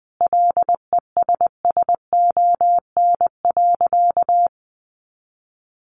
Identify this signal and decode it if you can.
Morse code, heard as a keyed tone, about 20 wpm, 700 Hz. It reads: LESSON.
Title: Sentences from Top 1200 Words 20wpm